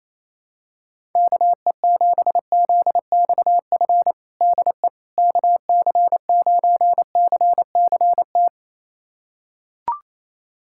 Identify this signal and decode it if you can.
Morse code, heard as a keyed tone, about 28 wpm, 700 Hz. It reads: KE7ZXF DE KC9CCT
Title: USA Callsign DE Callsign - 6 Characters - 28wpm